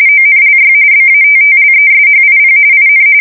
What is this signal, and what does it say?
RTTY, about 45.45 baud. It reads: ONLY BATTLESHIP IN
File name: fsk170 / battleshipnj